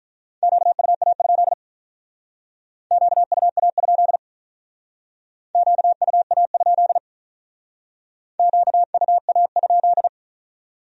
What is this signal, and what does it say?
Morse code, heard as a keyed tone, about 44 wpm, 700 Hz. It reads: QUA? QUA? QUA? QUA?
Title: Speed-Racing Q-Codes - 35wpm